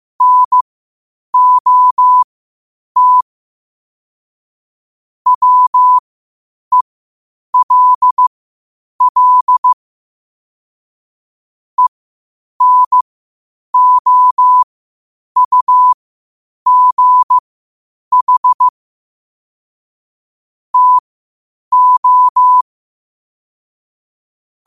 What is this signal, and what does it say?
Morse code, heard as a keyed tone, about 15 wpm, spaced as for 8 wpm, 1000 Hz. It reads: NOT WELL ENOUGH TO
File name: qotdfiles / 8wpm